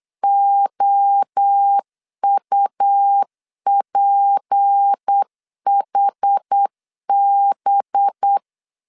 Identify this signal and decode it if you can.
Morse code, heard as a keyed tone, about 8 wpm, 800 Hz. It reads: OUPHB
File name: CW / DET / 1 6 1500C S B 8wpm 5142